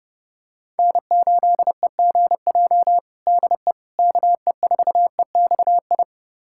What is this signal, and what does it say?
Morse code, heard as a keyed tone, about 30 wpm, 700 Hz. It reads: N8EGJ DE KE4EXI
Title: USA Callsign DE Callsign 30wpm